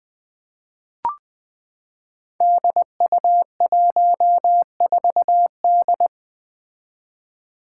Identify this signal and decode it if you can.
Morse code, heard as a keyed tone, about 20 wpm, 700 Hz. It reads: DU14D
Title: Five Challenging Letter-Number Combinations 20wpm